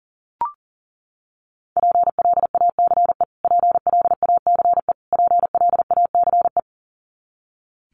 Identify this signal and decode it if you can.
Morse code, heard as a keyed tone, about 40 wpm, 700 Hz. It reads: PLACE PLACE PLACE
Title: Top 200 Words Repeated 40wpm